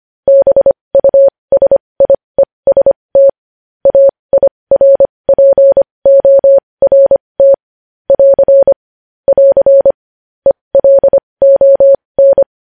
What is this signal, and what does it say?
Morse code, heard as a keyed tone, about 25 wpm, spaced as for 20 wpm, 550 Hz. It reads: BUSIEST AIRPORT <AR> <AR> ELON
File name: podcast / News headlines-1x-25x20